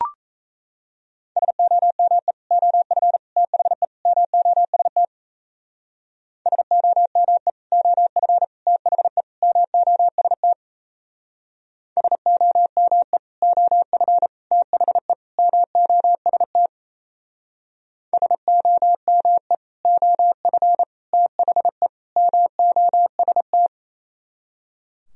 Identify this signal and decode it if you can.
Morse code, heard as a keyed tone, about 36 wpm, 700 Hz. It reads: SOME OF THE MOST SOME OF THE MOST SOME OF THE MOST SOME OF THE MOST